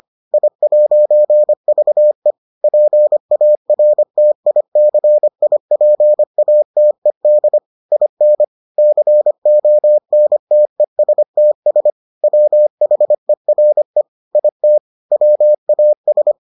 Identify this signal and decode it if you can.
Morse code, heard as a keyed tone, about 25 wpm, 600 Hz. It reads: I'VE PARTICIPATED IN CONTESTS WHERE IT WAS